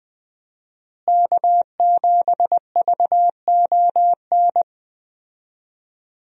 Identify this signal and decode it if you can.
Morse code, heard as a keyed tone, about 20 wpm, 700 Hz. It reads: K7VON
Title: USA Call Signs - 5 Characters - Repeated 2x - 20wpm